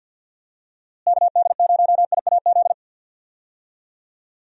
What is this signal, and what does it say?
Morse code, heard as a keyed tone, about 52 wpm, 700 Hz. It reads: KD0IAZ
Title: USA Call Signs - 6 Characters - Repeated 2x - 50wpm